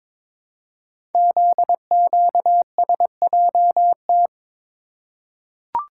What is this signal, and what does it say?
Morse code, heard as a keyed tone, about 22 wpm, 700 Hz. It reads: ZQSJT
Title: Code Groups - 5 Letters - 22wpm